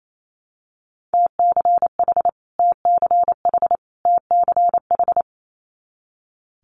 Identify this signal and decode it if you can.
Morse code, heard as a keyed tone, about 28 wpm, 700 Hz. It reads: TCH TCH TCH
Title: Multi-letter Phonemes Repeated 3x 28wpm